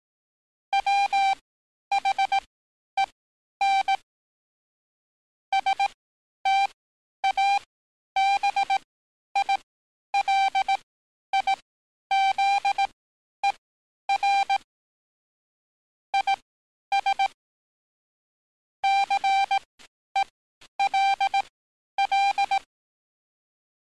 Morse code, heard as a keyed tone, about 18 wpm, spaced as for 10 wpm, 790 Hz. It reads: WHEN STABILIZER IS CELL
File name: training / NAVPERS 11049to11060 / NAVPERS 11050 RA RB